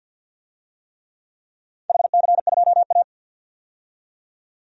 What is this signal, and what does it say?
Morse code, heard as a keyed tone, about 46 wpm, 700 Hz. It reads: HC2A
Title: DXCC-100 Call Signs - 4 Characters - 50wpm